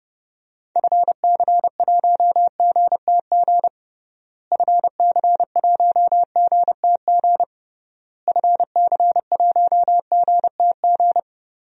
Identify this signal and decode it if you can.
Morse code, heard as a keyed tone, about 29 wpm, 700 Hz. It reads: FC1GTG FC1GTG FC1GTG
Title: DXCC-100 Call Signs - 6 Characters - Repeated 3x - 30wpm